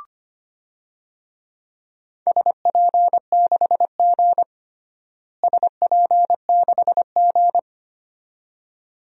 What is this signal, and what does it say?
Morse code, heard as a keyed tone, about 25 wpm, 700 Hz. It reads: SP6G SP6G
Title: DXCC-100 Call Signs - 4 Characters - Repeated 2x - 25wpm